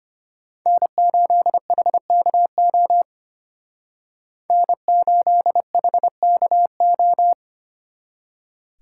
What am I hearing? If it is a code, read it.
Morse code, heard as a keyed tone, about 32 wpm, 700 Hz. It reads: N8HKO N8HKO